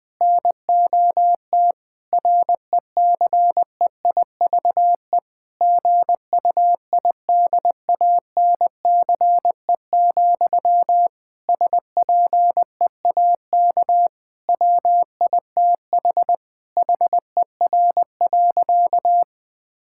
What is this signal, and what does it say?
Morse code, heard as a keyed tone, about 20 wpm, 700 Hz. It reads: NOT RECEIVE GUIDANCE, SPEAK WITH HER.
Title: Sentences from Top 1600 Words 20wpm